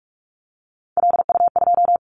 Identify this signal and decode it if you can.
Morse code, heard as a keyed tone, about 46 wpm, 700 Hz. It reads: LU2
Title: Callsign Prefixes for Top 100 DXCC - 45wpm